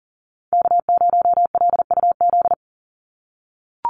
Morse code, heard as a keyed tone, about 39 wpm, 700 Hz. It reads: K0LUZ
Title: Call Signs - 40wpm